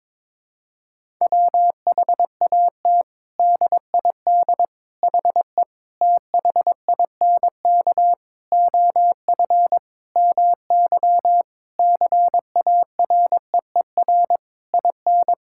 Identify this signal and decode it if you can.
Morse code, heard as a keyed tone, about 22 wpm, 700 Hz. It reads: WHAT DID HE THINK OF MY CAREER IN
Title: Sentences from Top 1000 Words - Review Part 6 - 22wpm